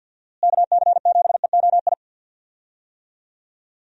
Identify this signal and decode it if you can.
Morse code, heard as a keyed tone, about 53 wpm, 700 Hz. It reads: KC7EOI
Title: USA Call Signs - 6 Characters - Repeated 2x - 50wpm